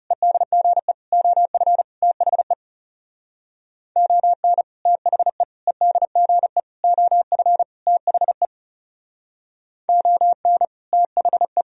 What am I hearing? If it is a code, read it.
Morse code, heard as a keyed tone, about 38 wpm, 700 Hz. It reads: EDGE OF THE ON THE EDGE OF THE ON THE